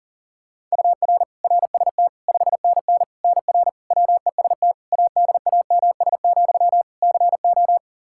Morse code, heard as a keyed tone, about 40 wpm, 700 Hz. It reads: UR RST 5NN NR WEST ADAMS, CO
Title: Practice QSO - Exchange 1 - Beginning - 40wpm